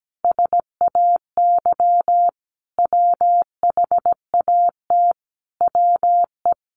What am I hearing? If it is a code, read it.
Morse code, heard as a keyed tone, about 17 wpm, 700 Hz. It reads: SAY WHAT WE
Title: Sentences from Top 1200 Words 17wpm